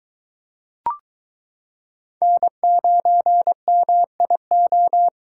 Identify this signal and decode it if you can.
Morse code, heard as a keyed tone, about 22 wpm, 700 Hz. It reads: N9MIO